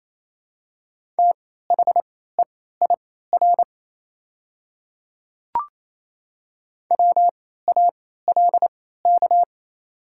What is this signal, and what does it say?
Morse code, heard as a keyed tone, about 27 wpm, spaced as for 16 wpm, 700 Hz. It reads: THEIR WALK
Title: 3x Character Spacing - Top 300 Words - 28wpm